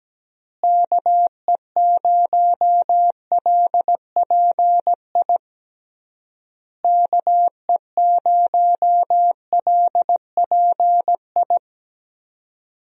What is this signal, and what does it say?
Morse code, heard as a keyed tone, about 17 wpm, 700 Hz. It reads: KE0LPI KE0LPI
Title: USA Call Signs - 6 Characters - Repeated 3x - 17wpm